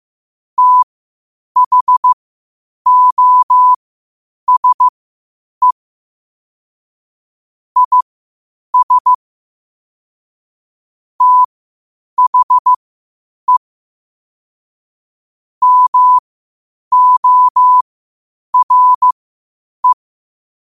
Morse code, heard as a keyed tone, about 15 wpm, spaced as for 8 wpm, 1000 Hz. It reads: THOSE IS THE MORE
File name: qotdfiles / 8wpm